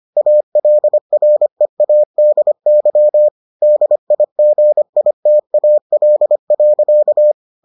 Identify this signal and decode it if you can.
Morse code, heard as a keyed tone, about 25 wpm, 600 Hz. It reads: ALREADY DIGITAL.